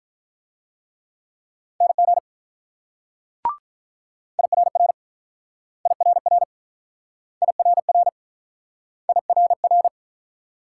Morse code, heard as a keyed tone, about 48 wpm, 700 Hz. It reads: NG IRR IRR IRR IRR